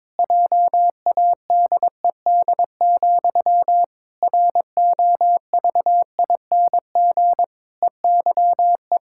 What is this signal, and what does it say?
Morse code, heard as a keyed tone, about 22 wpm, 700 Hz. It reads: JADED, ROVING EYE
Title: Pangram Challenge v2 - 22wpm